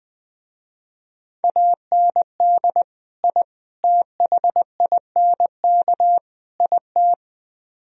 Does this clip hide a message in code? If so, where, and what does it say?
Yes: Morse code from 1.4 to 7.1 s, heard as a keyed tone, about 20 wpm, 700 Hz. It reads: AND I THINK IT